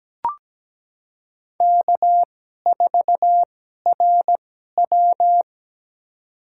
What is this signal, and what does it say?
Morse code, heard as a keyed tone, about 17 wpm, spaced as for 12 wpm, 700 Hz. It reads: K4RW
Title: USA Call Signs - 4 Characters - 2x Character Spacing - 17wpm